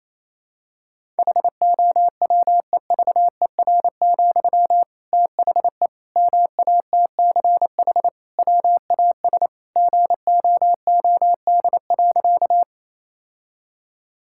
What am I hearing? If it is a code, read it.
Morse code, heard as a keyed tone, about 27 wpm, 700 Hz. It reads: HOWEVER, THE MATCH WAS GOOD.
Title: Sentences from Top 1600 Words 28wpm